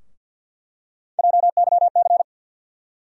Morse code, heard as a keyed tone, about 51 wpm, 700 Hz. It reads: JYC
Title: USA Call Sign Postfixes - 3 Letters 50wpm